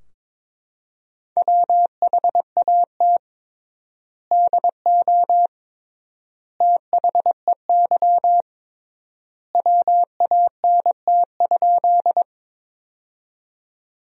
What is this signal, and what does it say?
Morse code, heard as a keyed tone, about 22 wpm, 700 Hz. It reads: WHAT DO THEY WANT?
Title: 3x Word-Spacing - Sentences from Top 200 Words - 22wpm